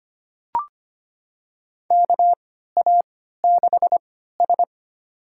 Morse code, heard as a keyed tone, about 25 wpm, spaced as for 14 wpm, 700 Hz. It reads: KA6S
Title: USA Call Signs - 4 Characters - 3x Character Spacing - 25wpm